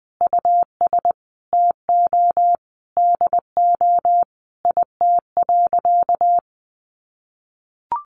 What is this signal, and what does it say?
Morse code, heard as a keyed tone, about 19 wpm, 700 Hz. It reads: US TO DO IT.
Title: Sentences from Top 1500 Words 20wpm